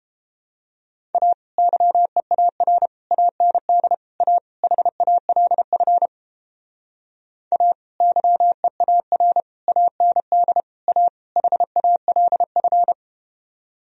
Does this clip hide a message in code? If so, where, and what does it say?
Yes: Morse code from 1.1 to 12.9 s, heard as a keyed tone, about 31 wpm, 700 Hz. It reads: A YEAR AND A HALF A YEAR AND A HALF